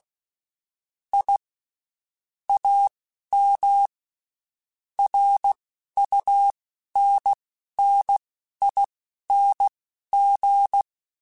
Morse code, heard as a keyed tone, about 16 wpm, spaced as for 11 wpm, 800 Hz. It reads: I AM RUNNING
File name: (spaced) / QSOs Spaced-mp3s / WA7QOX 2 @15 sp